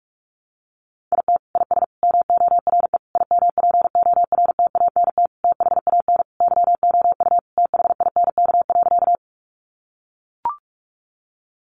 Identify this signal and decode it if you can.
Morse code, heard as a keyed tone, about 44 wpm, 700 Hz. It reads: IT IS MORE IMPORTANT THAN YOU THINK.
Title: Sentences from Top 600 Words 45wpm